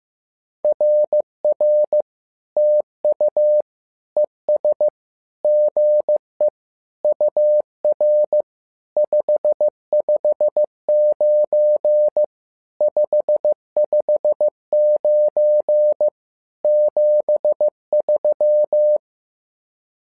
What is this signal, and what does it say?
Morse code, heard as a keyed tone, about 15 wpm, 600 Hz. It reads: RR TU ES GE UR 559 559 73
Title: SOTA QSO Practice 15wpm